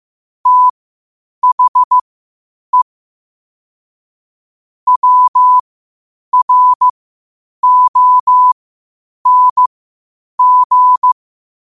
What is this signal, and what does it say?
Morse code, heard as a keyed tone, about 15 wpm, spaced as for 8 wpm, 1000 Hz. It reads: THE WRONG
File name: qotdfiles / 8wpm